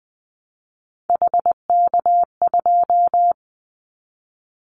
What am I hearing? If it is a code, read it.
Morse code, heard as a keyed tone, about 20 wpm, 700 Hz. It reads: HK2
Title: Callsign Prefixes for Top 100 DXCC - 20wpm